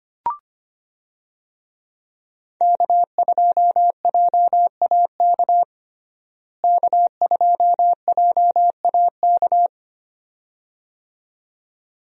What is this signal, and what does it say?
Morse code, heard as a keyed tone, about 25 wpm, 700 Hz. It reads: K2JAK K2JAK
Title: USA Call Signs - 5 Characters - Repeated 2x - 25wpm